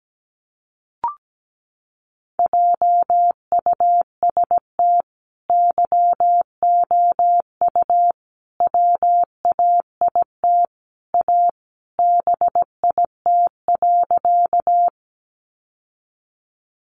Morse code, heard as a keyed tone, about 17 wpm, 700 Hz. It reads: JUST YOU WAIT A BIT.
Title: Sentences from Top 900 Words 17wpm